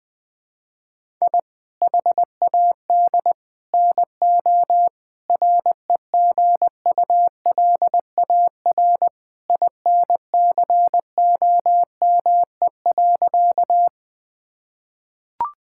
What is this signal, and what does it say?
Morse code, heard as a keyed tone, about 21 wpm, 700 Hz. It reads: I HAD NO REGULAR INCOME.
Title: Sentences from Top 1000 Words - Review Part 6 - 20wpm